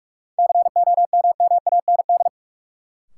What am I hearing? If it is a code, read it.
Morse code, heard as a keyed tone, about 42 wpm, 700 Hz. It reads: COMMAND
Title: Sentences from Top 1600 Words - New Word Drill Alt - 45wpm